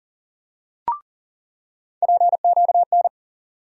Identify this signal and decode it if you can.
Morse code, heard as a keyed tone, about 40 wpm, 700 Hz. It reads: PQN
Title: USA Call Sign Postfixes - 3 Letters 40wpm